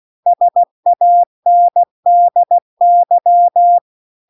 Morse code, heard as a keyed tone, about 16 wpm, 700 Hz. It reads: SANDY